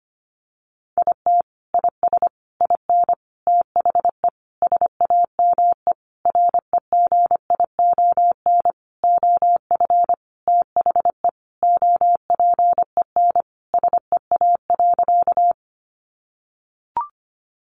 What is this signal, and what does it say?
Morse code, heard as a keyed tone, about 25 wpm, 700 Hz. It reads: IT IS IN THE SAME REGION OF THE OPEN SEA.
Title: Sentences from Top 1500 Words 25wpm